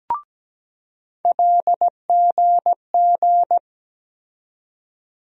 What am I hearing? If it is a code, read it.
Morse code, heard as a keyed tone, about 17 wpm, 700 Hz. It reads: LGG
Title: USA Call Sign Postfixes - 3 Letters 17wpm